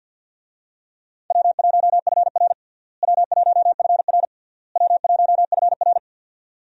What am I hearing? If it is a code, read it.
Morse code, heard as a keyed tone, about 48 wpm, 700 Hz. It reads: W1FR W1FR W1FR
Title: USA Call Signs - 4 Characters - Repeated 3x - 50wpm